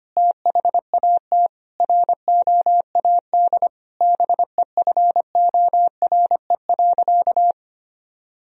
Morse code, heard as a keyed tone, about 25 wpm, 700 Hz. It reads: THAT ROAD BEFORE.
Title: Sentences from Top 1000 Words - Review Part 2 - 25wpm